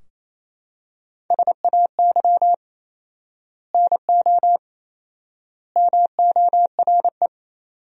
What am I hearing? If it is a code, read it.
Morse code, heard as a keyed tone, about 28 wpm, 700 Hz. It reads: SAY NO MORE
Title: 4x Word-Spacing - Sentences from Top 200 Words - 28wpm